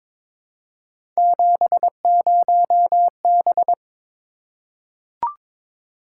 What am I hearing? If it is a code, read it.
Morse code, heard as a keyed tone, about 22 wpm, 700 Hz. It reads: 70B
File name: practice / 22wpm / Three Letter-Number Combinations - 22wpm